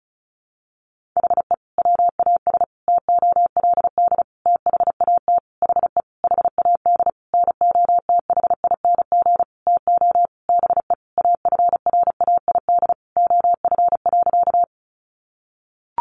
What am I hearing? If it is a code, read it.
Morse code, heard as a keyed tone, about 34 wpm, 700 Hz. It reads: HE WAS TOLD THAT HE HAD NOTHING TO BE AFRAID OF.
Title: Sentences from Top 1500 Words 35wpm